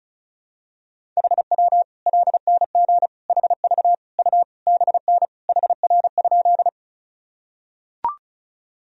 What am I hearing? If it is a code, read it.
Morse code, heard as a keyed tone, about 34 wpm, 700 Hz. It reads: HW LNG HV U BN HR?
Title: Sentences from Top 200 Words and Abbreviations 35wpm